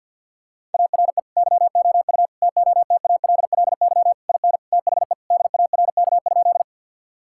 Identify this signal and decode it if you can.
Morse code, heard as a keyed tone, about 50 wpm, 700 Hz. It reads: ARE YOU TOTALLY IN THE DARK?
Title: Sentences from Top 1600 Words 50wpm